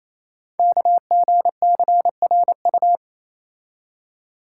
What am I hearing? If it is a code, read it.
Morse code, heard as a keyed tone, about 29 wpm, 700 Hz. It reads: KGCRU
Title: Code Groups - 5 Letters - 28wpm